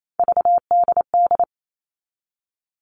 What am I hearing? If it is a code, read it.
Morse code, heard as a keyed tone, about 28 wpm, 700 Hz. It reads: VDD
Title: All Cut Numbers - 3-Digit 28wpm